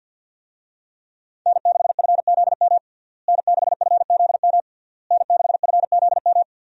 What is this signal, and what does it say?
Morse code, heard as a keyed tone, about 53 wpm, 700 Hz. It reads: N6FZM N6FZM N6FZM
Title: USA Call Signs - 5 Characters - Repeated 3x - 50wpm